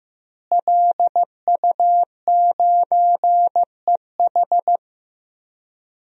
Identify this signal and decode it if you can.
Morse code, heard as a keyed tone, about 15 wpm, 700 Hz. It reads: LU9EH
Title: DXCC-100 Call Signs - 5 Characters - Repeated 3x - 15wpm